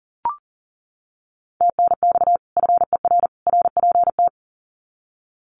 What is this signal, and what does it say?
Morse code, heard as a keyed tone, about 40 wpm, 700 Hz. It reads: TNX FER RPT
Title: Common QSO Phrases 40wpm